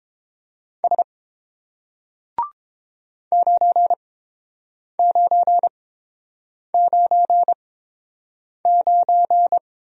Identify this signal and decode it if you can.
Morse code, heard as a keyed tone, about 30 wpm, spaced as for 7 wpm, 700 Hz. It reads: S 9999